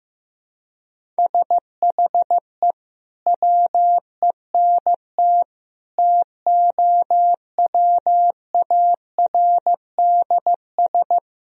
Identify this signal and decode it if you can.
Morse code, heard as a keyed tone, about 15 wpm, 700 Hz. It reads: SHE WENT TOWARDS
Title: Sentences from Top 1000 Words - Review Part 2 - 15wpm